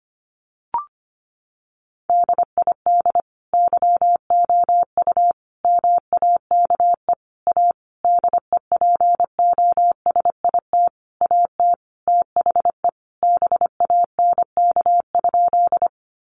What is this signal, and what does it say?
Morse code, heard as a keyed tone, about 25 wpm, 700 Hz. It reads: DID YOU MAKE A DEPOSIT AT THE BANK?
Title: Sentences from Top 1300 Words 25wpm